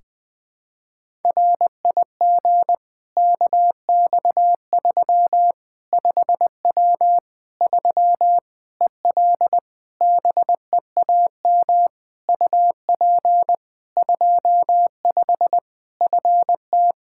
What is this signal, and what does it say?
Morse code, heard as a keyed tone, about 21 wpm, 700 Hz. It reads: RIG KX3 5W 3 EL BEAM UP 25 FT
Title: K1USN SST QSO Practice 20wpm